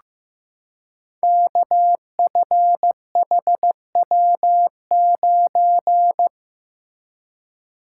Morse code, heard as a keyed tone, about 15 wpm, 700 Hz. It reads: KFHW9
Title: Code Groups - 5 Letter:Numbers - 15wpm